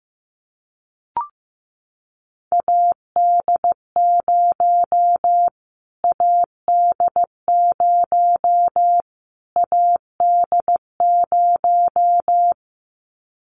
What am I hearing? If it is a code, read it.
Morse code, heard as a keyed tone, about 15 wpm, 700 Hz. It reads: AD0 AD0 AD0
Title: Callsign Prefixes for Top 25 DXCC - Repeated 3x - 15wpm